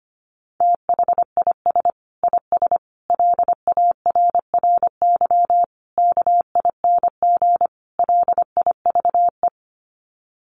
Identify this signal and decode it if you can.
Morse code, heard as a keyed tone, about 25 wpm, 700 Hz. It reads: THIS IS LARRY KING LIVE
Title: Sets of 5 Words Encore2 25wpm